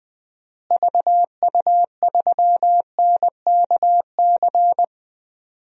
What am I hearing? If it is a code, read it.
Morse code, heard as a keyed tone, about 20 wpm, 700 Hz. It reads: VU3NKC
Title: DXCC-100 Call Signs - 6 Characters - Repeated 3x - 20wpm